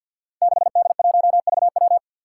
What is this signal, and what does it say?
Morse code, heard as a keyed tone, about 52 wpm, 700 Hz. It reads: 6D1VW
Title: Five Challenging Letter-Number Combinations 50wpm